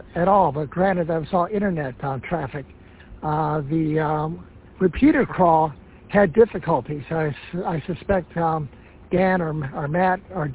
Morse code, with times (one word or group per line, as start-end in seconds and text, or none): none